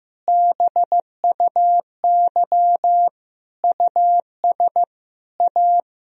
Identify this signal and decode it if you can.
Morse code, heard as a keyed tone, about 15 wpm, 700 Hz. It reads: BUY US A
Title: Sentences from Top 1000 Words - Review Part 4 - 15wpm